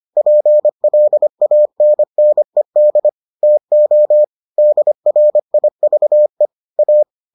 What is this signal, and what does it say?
Morse code, heard as a keyed tone, about 25 wpm, 600 Hz. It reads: PLANNED TO DRIVE A